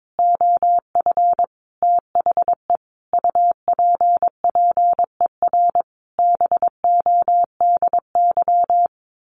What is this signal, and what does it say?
Morse code, heard as a keyed tone, about 22 wpm, 700 Hz. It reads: OF THE UPPER BODY
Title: A Sound Of Thunder 22wpm